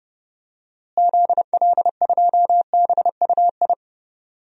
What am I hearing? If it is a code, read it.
Morse code, heard as a keyed tone, about 32 wpm, 700 Hz. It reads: ZL2BUI
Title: DXCC-100 Call Signs - 6 Characters - Repeated 2x - 30wpm